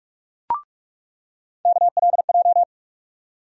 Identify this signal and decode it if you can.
Morse code, heard as a keyed tone, about 43 wpm, 700 Hz. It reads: KLJ
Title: USA Call Sign Postfixes - 3 Letters 45wpm